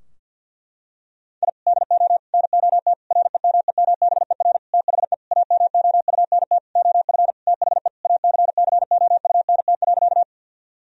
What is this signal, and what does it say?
Morse code, heard as a keyed tone, about 46 wpm, 700 Hz. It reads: I DO NOT REMEMBER THE AMOUNT OF THE ACCOUNT.
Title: Sentences from Top 1000 Words - Review Part 4 - 50wpm